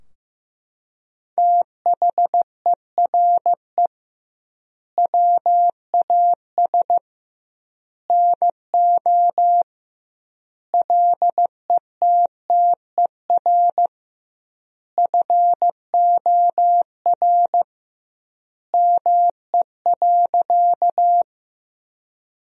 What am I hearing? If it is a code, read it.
Morse code, heard as a keyed tone, about 15 wpm, 700 Hz. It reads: THERE WAS NO LETTER FOR ME.
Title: 2x Word-Spacing - Sentences from Top 300 Words - 15wpm